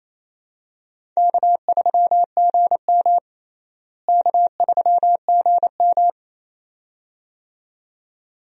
Morse code, heard as a keyed tone, about 28 wpm, 700 Hz. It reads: K3GM K3GM
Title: USA Call Signs - 4 Characters - Repeated 2x - 28wpm